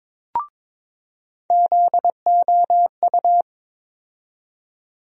Morse code, heard as a keyed tone, about 22 wpm, 700 Hz. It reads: ZOU
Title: USA Call Sign Postfixes - 3 Letters 22wpm